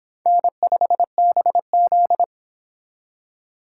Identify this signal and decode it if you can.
Morse code, heard as a keyed tone, about 25 wpm, 700 Hz. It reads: N5BZ